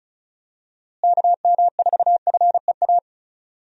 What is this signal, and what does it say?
Morse code, heard as a keyed tone, about 34 wpm, 700 Hz. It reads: KM4FEA
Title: USA Call Signs - 6 Characters - Repeated 3x - 35wpm